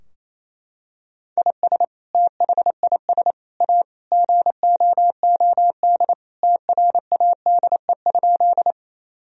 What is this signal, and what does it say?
Morse code, heard as a keyed tone, about 28 wpm, 700 Hz. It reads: IS THIS A GOOD TRADE?
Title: Sentences from Top 1000 Words - Review Part 4 - 28wpm